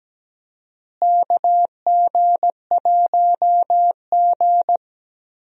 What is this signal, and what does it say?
Morse code, heard as a keyed tone, about 17 wpm, 700 Hz. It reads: KG1G
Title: USA Call Signs - 4 Characters - Repeated 2x - 17wpm